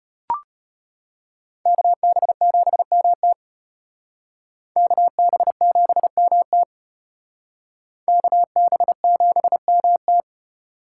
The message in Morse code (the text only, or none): KB7MT KB7MT KB7MT